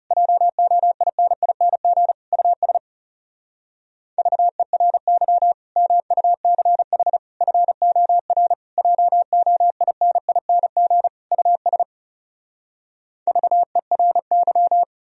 Morse code, heard as a keyed tone, about 40 wpm, 700 Hz. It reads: JOINING US VERY MUCH FOR JOINING US VERY